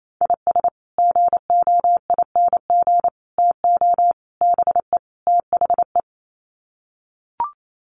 Morse code, heard as a keyed tone, about 28 wpm, 700 Hz. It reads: IS GOING TO BE THE
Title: Sets of 5 Words 28wpm